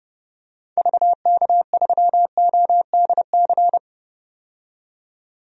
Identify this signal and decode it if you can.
Morse code, heard as a keyed tone, about 30 wpm, 700 Hz. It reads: VK3ODC
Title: DXCC-100 Call Signs - 6 Characters - 30wpm